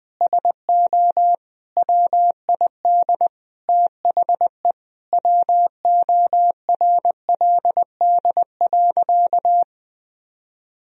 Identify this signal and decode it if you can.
Morse code, heard as a keyed tone, about 20 wpm, 700 Hz. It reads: SO WID THE WORLD.
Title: Sentences from Top 200 Words and Abbreviations 20wpm